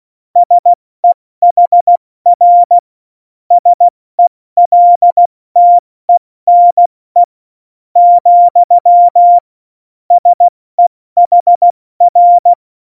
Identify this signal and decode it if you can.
Morse code, heard as a keyed tone, about 16 wpm, 700 Hz. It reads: SEHR SELTENE , SEHR
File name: Morsekurs / Witze / Witz 02. Januar 2018, 70 F80